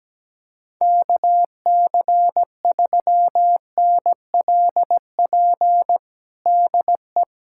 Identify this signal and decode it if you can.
Morse code, heard as a keyed tone, about 17 wpm, 700 Hz. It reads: KC3NLP DE
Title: USA Callsign DE Callsign 17wpm